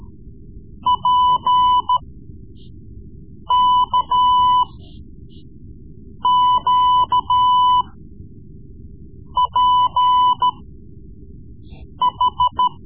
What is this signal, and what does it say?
Morse code, heard as a keyed tone, about 14 wpm, spaced as for 5 wpm, 1000 Hz. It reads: PKQPH
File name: Morse Lessons / Morse Lessons / Tape-3B